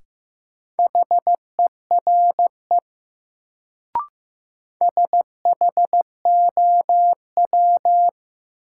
Morse code, heard as a keyed tone, about 15 wpm, 700 Hz. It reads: HERE SHOW
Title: Sentences from Top 200 Words - New Word Drill - 15wpm